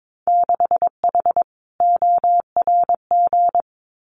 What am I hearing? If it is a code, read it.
Morse code, heard as a keyed tone, about 22 wpm, 700 Hz. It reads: 6H ORG
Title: Winter Field Day Exchanges Encore 22wpm